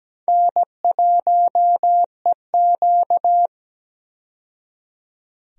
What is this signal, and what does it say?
Morse code, heard as a keyed tone, about 17 wpm, 700 Hz. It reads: N1EQ
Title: USA Call Signs - 4 Characters - 17wpm